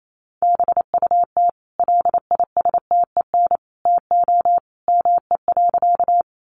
Morse code, heard as a keyed tone, about 28 wpm, 700 Hz. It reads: BUT LISTEN TO ME.
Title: Sentences from Top 400 Words 28wpm